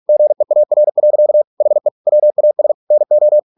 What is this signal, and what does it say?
Morse code, heard as a keyed tone, about 46 wpm, 600 Hz. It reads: GEAR. HE WAS NO